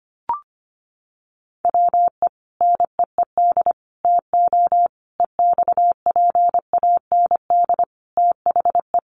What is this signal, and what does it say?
Morse code, heard as a keyed tone, about 25 wpm, 700 Hz. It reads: WE NEED TO EXPAND THE
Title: Sentences from Top 1400 Words 25wpm